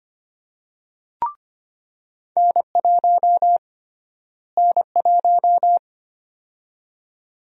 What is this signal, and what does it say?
Morse code, heard as a keyed tone, about 25 wpm, 700 Hz. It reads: N1 N1
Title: USA Call Sign Prefixes - Repeated 2x - 25wpm